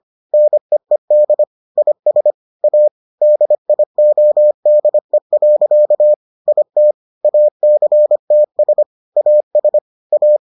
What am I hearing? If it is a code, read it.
Morse code, heard as a keyed tone, about 25 wpm, 600 Hz. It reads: NEED IS A DIODE. IT ACTS AS A